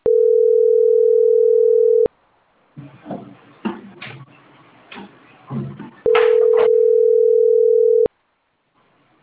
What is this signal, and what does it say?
Ringback tone (North America), 440 + 480 Hz.